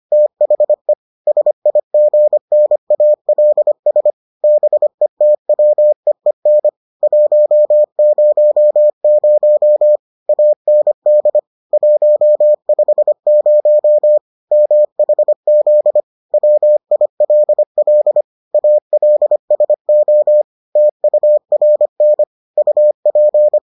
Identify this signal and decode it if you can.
Morse code, heard as a keyed tone, about 25 wpm, 600 Hz. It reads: THE SIGNALS BETWEEN 100 AND 150 MHZ WILL ALSO TURN UP